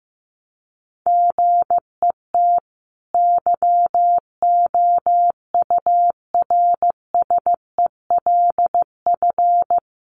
Morse code, heard as a keyed tone, about 15 wpm, 700 Hz. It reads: GET YOURSELF
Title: Sentences from Top 1100 Words 15wpm